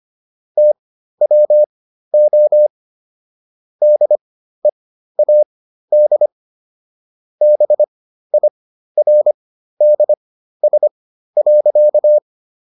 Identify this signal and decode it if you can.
Morse code, heard as a keyed tone, about 25 wpm, spaced as for 13 wpm, 600 Hz. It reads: TWO DEAD BIRDS.